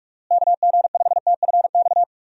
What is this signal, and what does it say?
Morse code, heard as a keyed tone, about 46 wpm, 700 Hz. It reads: KG5TFX
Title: USA Call Signs - 6 Characters - Repeated 2x - 45wpm